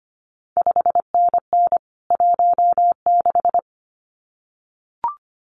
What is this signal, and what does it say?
Morse code, heard as a keyed tone, about 25 wpm, 700 Hz. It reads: 5NN 16
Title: CQ WW DX Contest Exchange - 25wpm